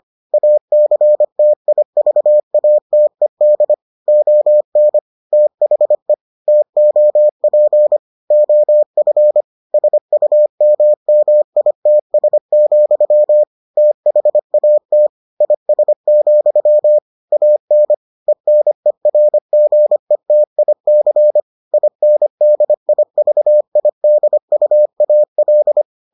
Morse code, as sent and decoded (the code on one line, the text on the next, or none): .- -.-. - .. ...- .- - . -.. / --- -. / - .... / - --- .--. / --- ..-. / ... ..- -- -- .. - ... --..-- / - .... .- - / .. ... --..-- / .- -. / . -. .-. --. - .. -.-. / .. -. -.. .. ...- .. -.. ..- .- .-..
ACTIVATED ON THE TOP OF SUMMITS, THAT IS, AN ENERGETIC INDIVIDUAL